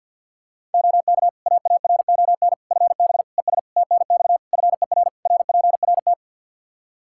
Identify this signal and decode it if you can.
Morse code, heard as a keyed tone, about 52 wpm, 700 Hz. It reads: OK AARON FB ES TNX FER RPRT
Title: K1USN SST QSO Practice 50wpm